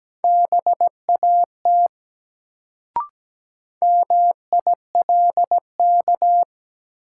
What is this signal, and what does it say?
Morse code, heard as a keyed tone, about 17 wpm, 700 Hz. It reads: BAT MILK
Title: Sentences from Top 1500 Words - New Word Drill Alt - 17wpm